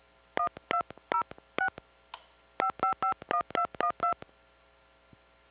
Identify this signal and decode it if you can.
DTMF, heard as telephone touch tones, tones of about 90 ms. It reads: 1206 5551212